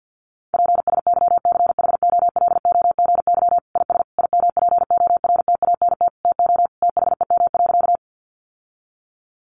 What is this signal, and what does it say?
Morse code, heard as a keyed tone, about 50 wpm, 700 Hz. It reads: PSYCHOLOGY IS IMPORTANT TO THEM.
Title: Sentences from Top 1400 Words 50wpm